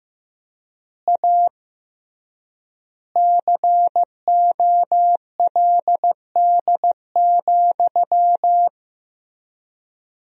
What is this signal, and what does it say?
Morse code, heard as a keyed tone, about 16 wpm, 700 Hz. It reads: A COLD,
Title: 3x Word-Spacing - Sentences from Top 500 Words - 15wpm